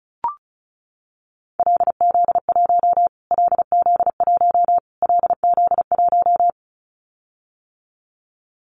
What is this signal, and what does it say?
Morse code, heard as a keyed tone, about 34 wpm, 700 Hz. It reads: LZ1 LZ1 LZ1
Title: Callsign Prefixes for Top 25 DXCC - Repeated 3x - 35wpm